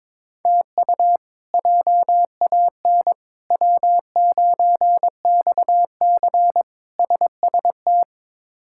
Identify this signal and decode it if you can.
Morse code, heard as a keyed tone, about 22 wpm, 700 Hz. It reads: TU JAN W9XC SST
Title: K1USN SST QSO Practice - 22wpm